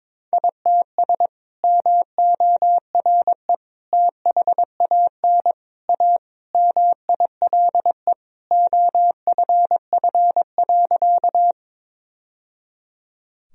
Morse code, heard as a keyed tone, about 21 wpm, 700 Hz. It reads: ITS MORE THAN A MILE OFF.
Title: Sentences from Top 1000 Words - Review Part 1 - 22wpm